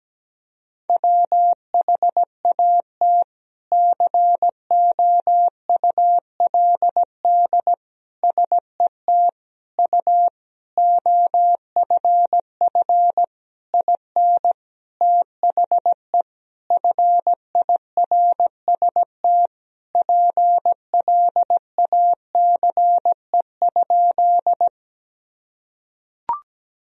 Morse code, heard as a keyed tone, about 17 wpm, 700 Hz. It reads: WHAT COULD SET U OFF IN THE FIRST PLACE?
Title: Sentences from Top 200 Words and Abbreviations 17wpm